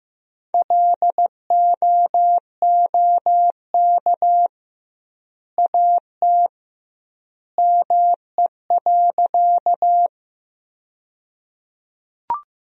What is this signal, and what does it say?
Morse code, heard as a keyed tone, about 15 wpm, 700 Hz. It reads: LOOK AT ME.
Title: 2x Word-Spacing - Sentences from Top 200 Words - 15wpm